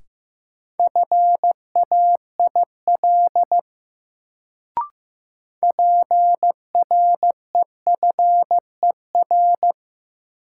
Morse code, heard as a keyed tone, about 15 wpm, 700 Hz. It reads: FAIL PREFER
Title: Sentences from Top 1100 Words - New Word Drill - 15wpm